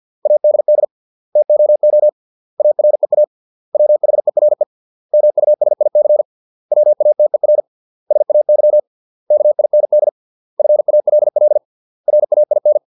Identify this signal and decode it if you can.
Morse code, heard as a keyed tone, about 50 wpm, 600 Hz. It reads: ADD TOO AREA WHERE MUSIC WATER SAY KIND FALL RAIN